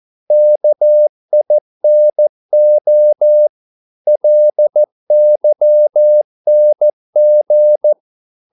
Morse code, heard as a keyed tone, about 14 wpm, 600 Hz. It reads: KINO LYNG
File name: Korte ord DK 14wpm